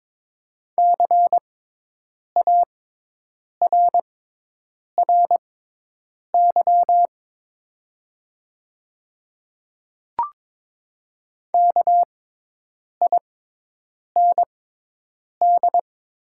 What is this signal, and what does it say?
Morse code, heard as a keyed tone, about 22 wpm, spaced as for 8 wpm, 700 Hz. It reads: CARRY KIND
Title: 6x Character Spacing - Top 500 Words - 22wpm